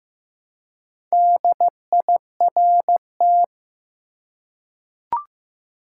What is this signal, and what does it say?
Morse code, heard as a keyed tone, about 15 wpm, 700 Hz. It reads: DIRT
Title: Four Letter Words - 15wpm